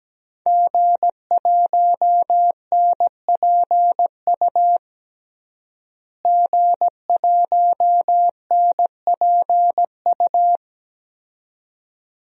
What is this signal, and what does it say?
Morse code, heard as a keyed tone, about 17 wpm, 700 Hz. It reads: G1NPU G1NPU
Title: DXCC-100 Call Signs - 5 Characters - Repeated 2x - 17wpm